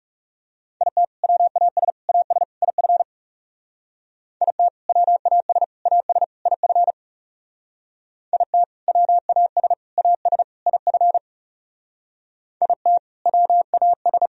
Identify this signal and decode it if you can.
Morse code, heard as a keyed tone, about 36 wpm, 700 Hz. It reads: IT WAS AS IF IT WAS AS IF IT WAS AS IF IT WAS
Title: Speed-Racing - Sets of 4 Words 30wpm